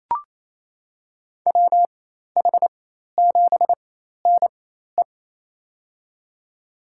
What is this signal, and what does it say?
Morse code, heard as a keyed tone, about 27 wpm, spaced as for 13 wpm, 700 Hz. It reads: WH7NE
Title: USA Call Signs - 5 Characters - 4x Character Spacing - 28wpm